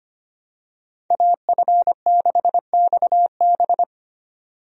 Morse code, heard as a keyed tone, about 25 wpm, 700 Hz. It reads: AF6XB
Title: USA Call Signs - 5 Characters - 25wpm